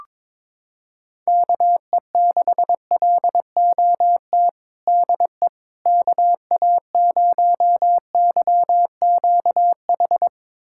Morse code, heard as a keyed tone, about 22 wpm, 700 Hz. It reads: KE6LOT DE KA0YQH
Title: USA Callsign DE Callsign 22wpm